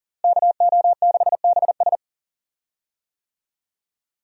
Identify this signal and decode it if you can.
Morse code, heard as a keyed tone, about 38 wpm, 700 Hz. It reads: KO6BS
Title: USA Call Signs - 5 Characters - Repeated 3x - 40wpm